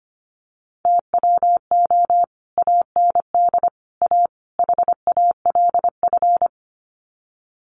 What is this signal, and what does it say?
Morse code, heard as a keyed tone, about 25 wpm, 700 Hz. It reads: TWO AND A HALF
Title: Sets of 4 Words 25wpm